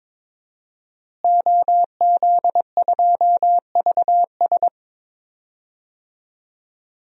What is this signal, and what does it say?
Morse code, heard as a keyed tone, about 22 wpm, 700 Hz. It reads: OZ2VS
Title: DXCC-100 Call Signs - 5 Characters - Repeated 2x - 22wpm